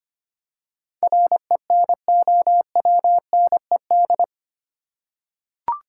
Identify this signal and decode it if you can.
Morse code, heard as a keyed tone, about 25 wpm, 700 Hz. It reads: RENOWNED
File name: 8 Letter Words - First 3 Letters Given - 25wpm